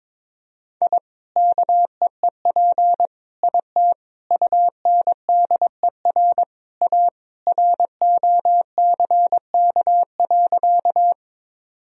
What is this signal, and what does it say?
Morse code, heard as a keyed tone, about 21 wpm, 700 Hz. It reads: I KEEP IT UNDER A ROCK.
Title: Sentences from Top 1000 Words - Review Part 2 - 22wpm